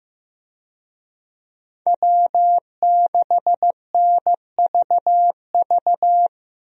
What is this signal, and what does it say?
Morse code, heard as a keyed tone, about 15 wpm, 700 Hz. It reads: W6NVV
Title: USA Call Signs - 5 Characters - Repeated 3x - 15wpm